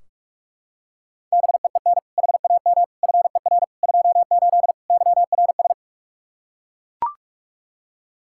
Morse code, heard as a keyed tone, about 41 wpm, 700 Hz. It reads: BEEN HAM FER 28 YRS
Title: Practice QSO - Exchange 3 - More Info - 45wpm